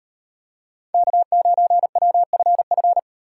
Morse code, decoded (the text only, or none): K9WFF